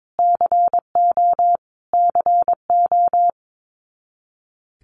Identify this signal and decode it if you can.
Morse code, heard as a keyed tone, about 22 wpm, 700 Hz. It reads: CO CO
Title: US States Repeated 3x 22wpm